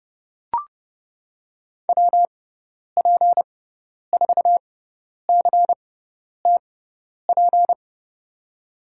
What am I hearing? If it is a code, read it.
Morse code, heard as a keyed tone, about 30 wpm, spaced as for 10 wpm, 700 Hz. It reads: WP4CTP
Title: USA Call Signs - 6 Characters - 6x Character Spacing - 30wpm